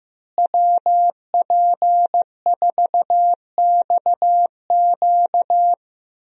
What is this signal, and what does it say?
Morse code, heard as a keyed tone, about 15 wpm, 700 Hz. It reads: WP4XQ
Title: USA Call Signs - 5 Characters - Repeated 2x - 15wpm